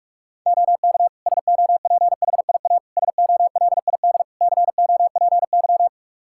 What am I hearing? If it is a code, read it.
Morse code, heard as a keyed tone, about 46 wpm, 700 Hz. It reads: OK SOPHIA SOLID COPY
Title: Practice QSO - Exchange 3 - More Info - 45wpm